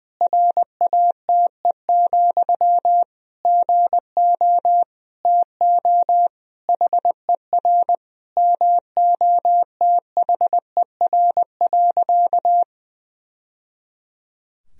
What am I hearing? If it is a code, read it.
Morse code, heard as a keyed tone, about 20 wpm, 700 Hz. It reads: RATE, GO TO HER MOTHER.
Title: Sentences from Top 1000 Words - Review Part 4 - 20wpm